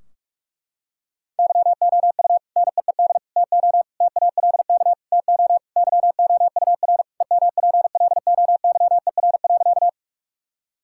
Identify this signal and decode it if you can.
Morse code, heard as a keyed tone, about 46 wpm, 700 Hz. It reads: YOU NEED TO TALK TO YOUR EMPLOYER.
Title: Sentences from Top 1600 Words 45wpm